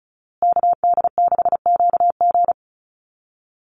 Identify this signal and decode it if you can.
Morse code, heard as a keyed tone, about 34 wpm, 700 Hz. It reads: KD6QG
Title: USA Call Signs Repeated 3x 35wpm